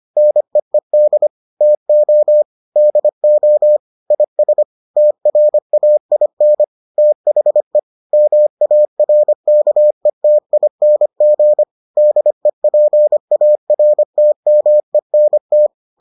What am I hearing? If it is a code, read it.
Morse code, heard as a keyed tone, about 25 wpm, 600 Hz. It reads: NEED TO DO IS TRAIN THE MARKETING DEPARTMENT